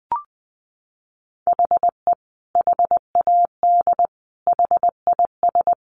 Morse code, heard as a keyed tone, about 20 wpm, 700 Hz. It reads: HE HAD HIS
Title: A Sound Of Thunder 20wpm